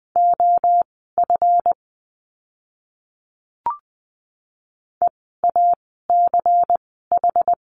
Morse code, heard as a keyed tone, about 20 wpm, spaced as for 14 wpm, 700 Hz. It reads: OF EACH
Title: 146 Lesson 24f - Letter y - Words 20wpm